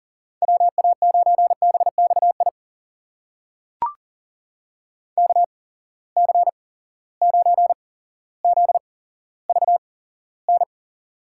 Morse code, heard as a keyed tone, about 40 wpm, spaced as for 17 wpm, 700 Hz. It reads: WA9BXI KC9ZVN
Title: USA Call Signs - 6 Characters - 8x Character Spacing - 40wpm